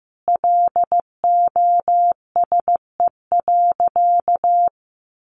Morse code, heard as a keyed tone, about 15 wpm, 700 Hz. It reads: LOSE.
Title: Sentences from Top 1100 Words 15wpm